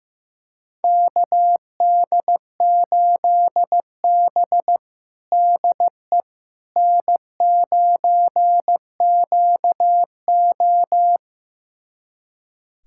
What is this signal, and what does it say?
Morse code, heard as a keyed tone, about 15 wpm, 700 Hz. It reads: KD8B DE N9QO
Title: USA Callsign DE Callsign - 4 Characters - 15wpm